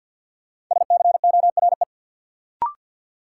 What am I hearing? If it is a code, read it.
Morse code, heard as a keyed tone, about 53 wpm, 700 Hz. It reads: SCORE
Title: Sentences from Top 1600 Words - New Word Drill - 50wpm